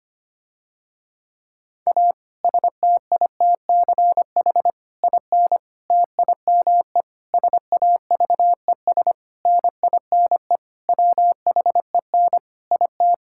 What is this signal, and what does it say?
Morse code, heard as a keyed tone, about 25 wpm, 700 Hz. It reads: A STITCH IN TIME SAVES NINE WHEN IT
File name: practice / 25wpm / Difficult Sentences with Hints - 25wpm